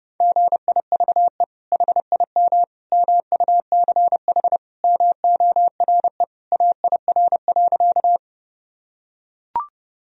Morse code, heard as a keyed tone, about 30 wpm, 700 Hz. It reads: GIVE HIM MUCH MORE AIR.
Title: Sentences from Top 1000 Words - Review Part 1 - 30wpm